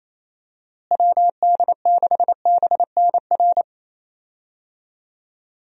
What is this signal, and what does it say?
Morse code, heard as a keyed tone, about 28 wpm, 700 Hz. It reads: WD6BNR
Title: USA Call Signs - 6 Characters - Repeated 3x - 28wpm